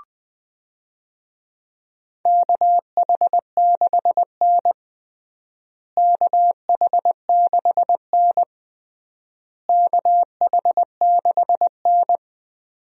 Morse code, heard as a keyed tone, about 21 wpm, 700 Hz. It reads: KH6N KH6N KH6N
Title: DXCC-100 Call Signs - 4 Characters - Repeated 3x - 20wpm